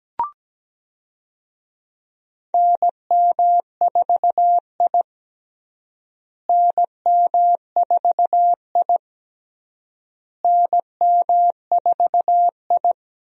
Morse code, heard as a keyed tone, about 17 wpm, 700 Hz. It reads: NM4I NM4I NM4I
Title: USA Call Signs - 4 Characters - Repeated 3x - 17wpm